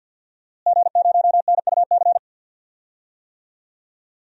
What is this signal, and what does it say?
Morse code, heard as a keyed tone, about 52 wpm, 700 Hz. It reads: G0NUC
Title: DXCC-100 Call Signs - 5 Characters - Repeated 2x - 50wpm